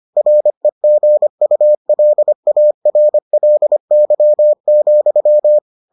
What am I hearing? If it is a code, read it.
Morse code, heard as a keyed tone, about 25 wpm, 600 Hz. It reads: REGULARLY,